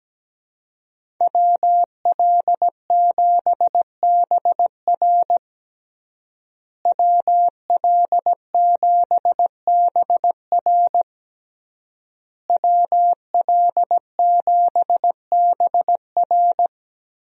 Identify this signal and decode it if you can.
Morse code, heard as a keyed tone, about 17 wpm, 700 Hz. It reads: WL7BR WL7BR WL7BR
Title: USA Call Signs - 5 Characters - Repeated 3x - 17wpm